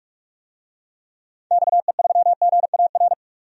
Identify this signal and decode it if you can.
Morse code, heard as a keyed tone, about 44 wpm, 700 Hz. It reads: XE3GAR
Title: DXCC-100 Call Signs - 6 Characters - 45wpm